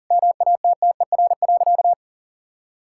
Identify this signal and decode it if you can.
Morse code, heard as a keyed tone, about 40 wpm, 700 Hz. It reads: MATTER.
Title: Sentences from Top 1000 Words - Review Part 5 - 40wpm